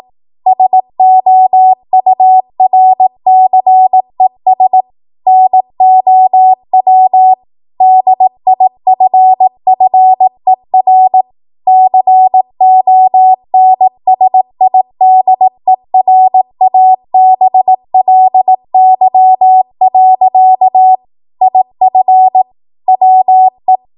Morse code, heard as a keyed tone, about 18 wpm, 750 Hz. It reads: SOURCES NOW DIFFER CONSIDERABLY. IF WE
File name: Morse / 18WPM